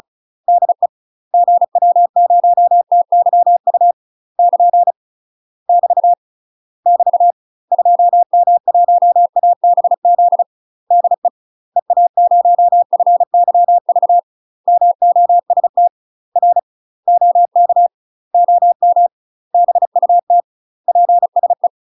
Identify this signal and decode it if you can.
Morse code, heard as a keyed tone, about 35 wpm, 700 Hz. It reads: DE GW0TYU <KN> <BT> <BT> 2M1ABZ DE EA0FYV MOST R OK OM BUT PSE
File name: UK MASTER QSOs TRAINING / 35WPM / UK MASTER QSOs 0039